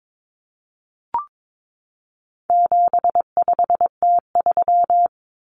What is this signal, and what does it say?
Morse code, heard as a keyed tone, about 22 wpm, 700 Hz. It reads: 75T3
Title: Common Cut Numbers - 4-Digit 22wpm